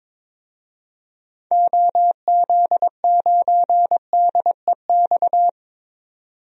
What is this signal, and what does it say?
Morse code, heard as a keyed tone, about 22 wpm, 700 Hz. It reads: OZ9DEX
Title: DXCC-100 Call Signs - 6 Characters - 22wpm